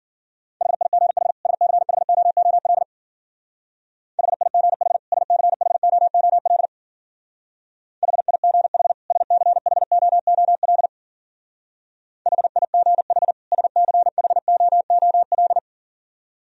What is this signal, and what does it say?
Morse code, heard as a keyed tone, about 49 wpm, 700 Hz. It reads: HIGH SCHOOL HIGH SCHOOL HIGH SCHOOL HIGH SCHOOL
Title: Speed-Racing - Sets of 2 Words 40wpm